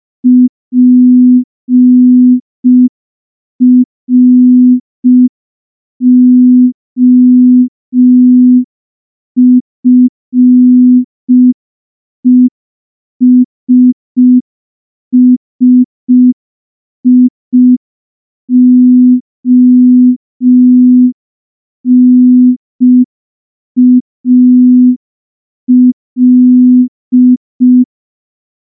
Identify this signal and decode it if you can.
Morse code, heard as a keyed tone, about 5 wpm, 250 Hz. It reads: PROFESSIONAL
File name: morse / morse-training-05WPM